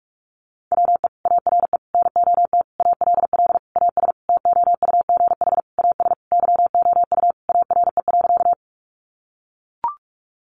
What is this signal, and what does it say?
Morse code, heard as a keyed tone, about 44 wpm, 700 Hz. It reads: WE ARE NOT ALL AS TOUGH AS YOU ARE.
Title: Sentences from Top 1200 Words 45wpm